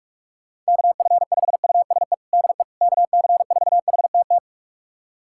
Morse code, heard as a keyed tone, about 42 wpm, 700 Hz. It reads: KF5USE DE KC4HTT
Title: USA Callsign DE Callsign 45wpm